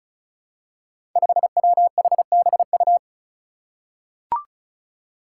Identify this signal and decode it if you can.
Morse code, heard as a keyed tone, about 36 wpm, 700 Hz. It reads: 5WHBU
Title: Five Challenging Letter-Number Combinations 35wpm